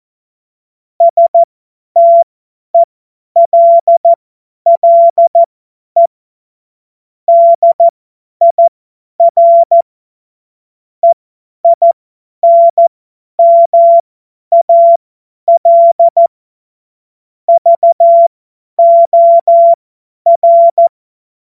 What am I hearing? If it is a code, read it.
Morse code, heard as a keyed tone, about 14 wpm, spaced as for 10 wpm, 680 Hz. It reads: STELLE DIR EINMAL VOR
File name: Morsekurs / Witze / Witz 29. Juni 2021, 50 F70